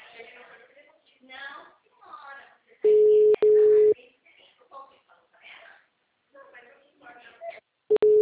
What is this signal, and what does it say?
Ringback tone (Europe, 425 Hz), with speech.